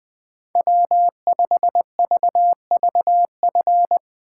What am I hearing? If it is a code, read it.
Morse code, heard as a keyed tone, about 20 wpm, 700 Hz. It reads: W5VVF
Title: Speed Racing - USA 1x3 Call Signs 17wpm